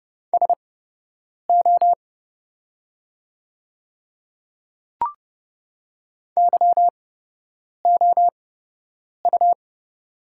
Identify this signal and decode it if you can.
Morse code, heard as a keyed tone, about 29 wpm, spaced as for 8 wpm, 700 Hz. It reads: SO YOU
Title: 8x Character Spacing - Top 100 Words - 30wpm